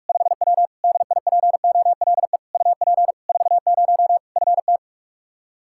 Morse code, heard as a keyed tone, about 46 wpm, 700 Hz. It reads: 5W DIPOLE UP 40 FT
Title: Practice QSO - Exchange 2 - Info - 45wpm